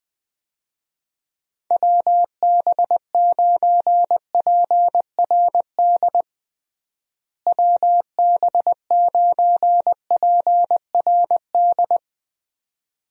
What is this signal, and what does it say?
Morse code, heard as a keyed tone, about 20 wpm, 700 Hz. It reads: WB9PRD WB9PRD
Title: USA Call Signs - 6 Characters - Repeated 2x - 20wpm